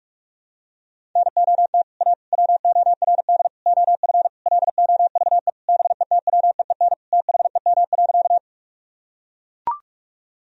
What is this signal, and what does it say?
Morse code, heard as a keyed tone, about 42 wpm, 700 Hz. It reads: NOT A WORD OF LOVE BETWEEN THEM.
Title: Sentences from Top 1000 Words - Review Part 1 - 45wpm